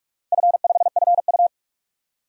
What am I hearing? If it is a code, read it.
Morse code, heard as a keyed tone, about 47 wpm, 700 Hz. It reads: F5FU
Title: DXCC-100 Call Signs - 4 Characters - Repeated 3x - 45wpm